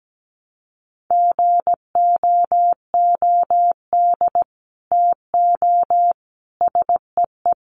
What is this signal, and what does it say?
Morse code, heard as a keyed tone, about 17 wpm, 700 Hz. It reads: GOOD TO SEE
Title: Sets of 4 Words 17wpm